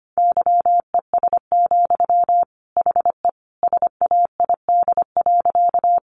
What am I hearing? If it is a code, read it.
Morse code, heard as a keyed tone, about 25 wpm, 700 Hz. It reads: YES, HE SAID.
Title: Sentences from Top 500 Words 25wpm